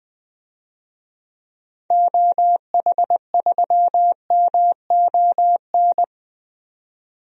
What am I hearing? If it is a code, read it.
Morse code, heard as a keyed tone, about 20 wpm, 700 Hz. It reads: OH3MON
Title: DXCC-100 Call Signs - 6 Characters - 20wpm